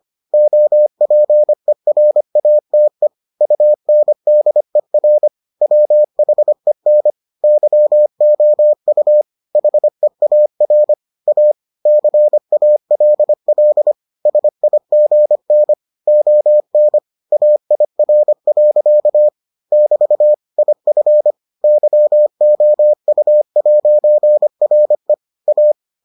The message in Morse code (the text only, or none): OPERATE UNDER WHEN YOU HEAR A CALL SIGN ON AIR. <BT> IF YOU'RE A